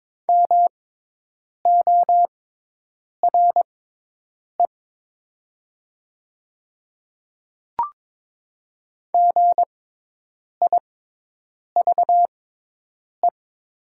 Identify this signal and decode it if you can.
Morse code, heard as a keyed tone, about 22 wpm, spaced as for 8 wpm, 700 Hz. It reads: MORE GIVE
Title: 6x Character Spacing - Top 300 Words - 22wpm